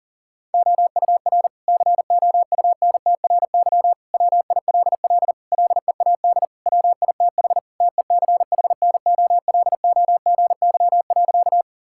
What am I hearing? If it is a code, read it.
Morse code, heard as a keyed tone, about 40 wpm, 700 Hz. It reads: OUR COUNTRY WILL LEAD WITH TECHNOLOGY.
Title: Sentences from Top 1000 Words - Review Part 5 - 40wpm